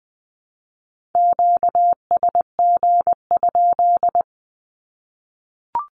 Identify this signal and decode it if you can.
Morse code, heard as a keyed tone, about 20 wpm, 700 Hz. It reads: QSG?
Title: Q-Codes 20wpm